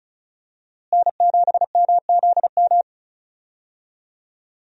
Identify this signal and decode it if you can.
Morse code, heard as a keyed tone, about 36 wpm, 700 Hz. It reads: N7MZM
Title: USA Call Signs - 5 Characters - Repeated 3x - 35wpm